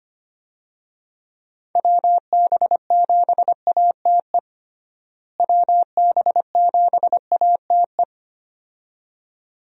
Morse code, heard as a keyed tone, about 25 wpm, 700 Hz. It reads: WB7ATE WB7ATE
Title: USA Call Signs - 6 Characters - Repeated 2x - 25wpm